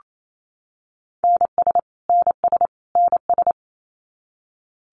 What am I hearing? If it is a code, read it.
Morse code, heard as a keyed tone, about 28 wpm, 700 Hz. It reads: NS NS NS
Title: Canadian Provinces and Territories - Repeated 3x - 28wpm